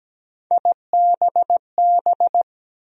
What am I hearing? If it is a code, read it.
Morse code, heard as a keyed tone, about 17 wpm, 700 Hz. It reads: IBB
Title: USA Call Sign Postfixes - 3 Letters 17wpm